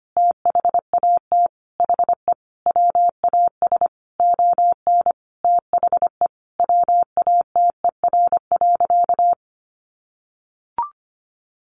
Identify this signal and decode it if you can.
Morse code, heard as a keyed tone, about 25 wpm, 700 Hz. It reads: THAT HE WAS ON THE WATER.
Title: Sentences from Top 500 Words 25wpm